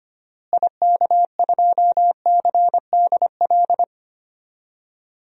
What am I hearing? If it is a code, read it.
Morse code, heard as a keyed tone, about 25 wpm, 700 Hz. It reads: IK2CDL
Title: DXCC-100 Call Signs - 6 Characters - Repeated 3x - 25wpm